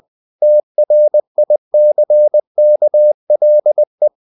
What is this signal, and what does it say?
Morse code, heard as a keyed tone, about 20 wpm, 600 Hz. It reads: TRICKLE
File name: the old man and the sea / the old man and the sea part 1 20wpm